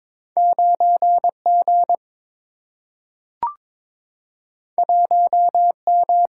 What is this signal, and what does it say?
Morse code, heard as a keyed tone, about 22 wpm, 700 Hz. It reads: 9G 1M
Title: Single-Letter Single-Number 22wpm